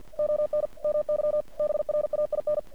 Morse code, heard as a keyed tone, about 48 wpm, 600 Hz. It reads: ON MY BRAIN
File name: archived / CRSnet